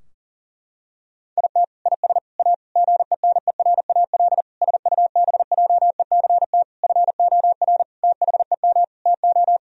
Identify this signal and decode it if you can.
Morse code, heard as a keyed tone, about 40 wpm, 700 Hz. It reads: IT IS A GENERAL SUBJECT FOR THEM TO
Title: Sentences from Top 1000 Words - Review Part 3 - 40wpm